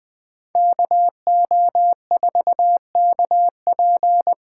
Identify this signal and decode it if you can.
Morse code, heard as a keyed tone, about 24 wpm, 700 Hz. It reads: KO4KP
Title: Speed Racing - USA 2x2 Call Signs 20wpm